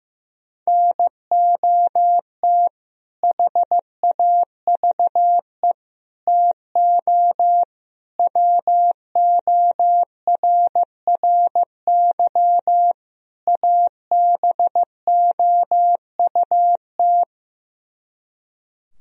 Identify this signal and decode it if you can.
Morse code, heard as a keyed tone, about 15 wpm, 700 Hz. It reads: NOT HAVE TO WORRY ABOUT
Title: Speed-Racing - Sets of 5 Words 15wpm